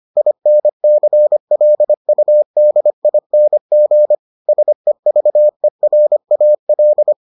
Morse code, heard as a keyed tone, about 25 wpm, 600 Hz. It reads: INCLUDING SEVERAL